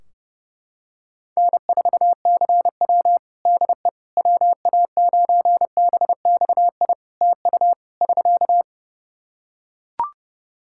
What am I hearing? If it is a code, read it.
Morse code, heard as a keyed tone, about 30 wpm, 700 Hz. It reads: N4CW DE WA9BXI TU <SK>
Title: Practice QSO - Exchange 4 - Ending - 30wpm